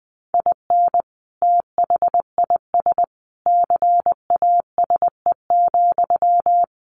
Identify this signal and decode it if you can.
Morse code, heard as a keyed tone, about 20 wpm, 700 Hz. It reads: IN THIS CASE,
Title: Sentences from Top 1400 Words 20wpm